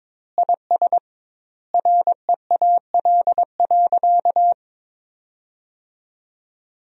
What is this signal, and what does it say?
Morse code, heard as a keyed tone, about 22 wpm, 700 Hz. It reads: IS REAL.
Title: 2x Word-Spacing - Sentences from Top 300 Words - 22wpm